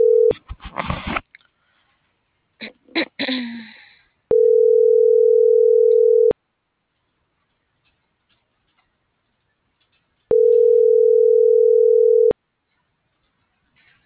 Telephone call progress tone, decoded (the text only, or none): ringback tone